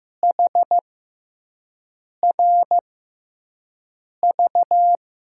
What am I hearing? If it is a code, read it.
Morse code, heard as a keyed tone, about 15 wpm, spaced as for 5 wpm, 700 Hz. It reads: HRV
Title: USA Call Signs - 6x Character Spacing - 15wpm